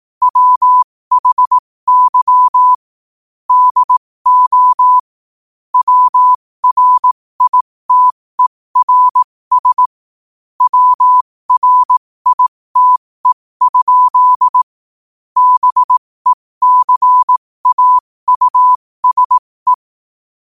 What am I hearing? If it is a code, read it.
Morse code, heard as a keyed tone, about 18 wpm, spaced as for 15 wpm, 1000 Hz. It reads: WHY DO WRITERS WRITE? BECAUSE
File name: qotdfiles / 15wpm